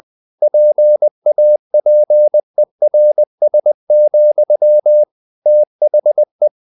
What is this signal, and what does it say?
Morse code, heard as a keyed tone, about 20 wpm, 600 Hz. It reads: PAPERS, THE
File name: the old man and the sea part 2 20wpm